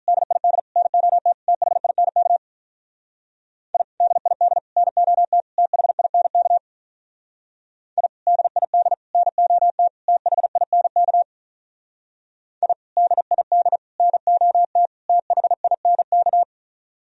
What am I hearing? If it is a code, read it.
Morse code, heard as a keyed tone, about 43 wpm, 700 Hz. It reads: DID NOT THINK I DID NOT THINK I DID NOT THINK I DID NOT THINK
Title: Speed-Racing - Sets of 4 Words 35wpm